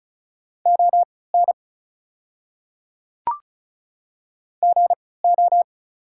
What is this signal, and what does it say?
Morse code, heard as a keyed tone, about 36 wpm, spaced as for 20 wpm, 690 Hz. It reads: ON GO
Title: 3x Character Spacing - Top 300 Words - 35wpm